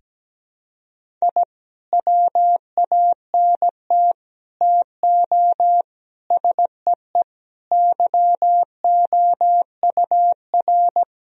Morse code, heard as a keyed tone, about 17 wpm, 700 Hz. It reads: I WANT TO SEE YOUR
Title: Sentences from Top 1000 Words - Review Part 1 - 17wpm